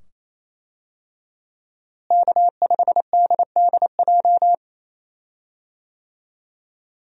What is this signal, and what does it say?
Morse code, heard as a keyed tone, about 28 wpm, 700 Hz. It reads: K5DDJ
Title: USA Call Signs - 5 Characters - Repeated 2x - 28wpm